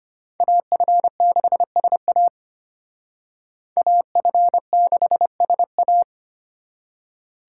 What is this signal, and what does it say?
Morse code, heard as a keyed tone, about 32 wpm, 700 Hz. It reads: AF6SA AF6SA